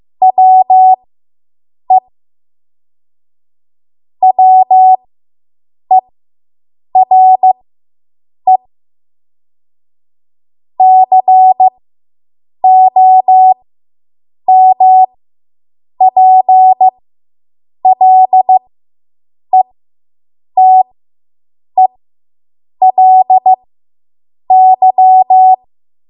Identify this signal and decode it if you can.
Morse code, heard as a keyed tone, about 15 wpm, spaced as for 7 wpm, 750 Hz. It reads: WE WERE COMPLETELY